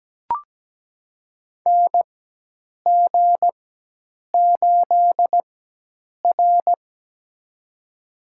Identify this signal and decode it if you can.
Morse code, heard as a keyed tone, about 17 wpm, spaced as for 8 wpm, 700 Hz. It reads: NG8R
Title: USA Call Signs - 4 Characters - 4x Character Spacing - 17wpm